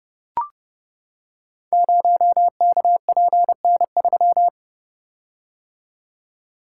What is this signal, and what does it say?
Morse code, heard as a keyed tone, about 30 wpm, 700 Hz. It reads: 0KPN3
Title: Code Groups - 5 Letter:Numbers - 30wpm